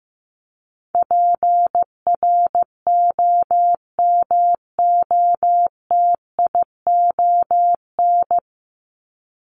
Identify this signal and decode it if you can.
Morse code, heard as a keyed tone, about 15 wpm, 700 Hz. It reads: PROMOTION
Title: Top 1501-1750 Words 15wpm